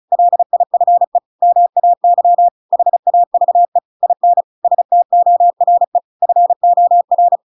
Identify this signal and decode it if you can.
Morse code, heard as a keyed tone, about 34 wpm, 700 Hz. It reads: LIFE MAY HAVE IN STORE FOR
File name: Pinocchio / Pinocchio ch7 35 35 ik1hge